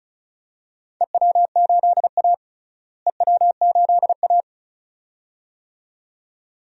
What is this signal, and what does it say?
Morse code, heard as a keyed tone, about 34 wpm, 700 Hz. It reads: EW8A EW8A
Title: DXCC-100 Call Signs - 4 Characters - Repeated 2x - 35wpm